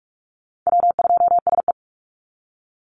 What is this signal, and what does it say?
Morse code, heard as a keyed tone, about 44 wpm, 690 Hz. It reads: W2SE
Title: NA SOTA Chaser Callsigns 45wpm